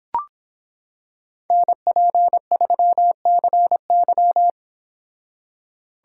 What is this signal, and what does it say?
Morse code, heard as a keyed tone, about 26 wpm, 700 Hz. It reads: NP3CY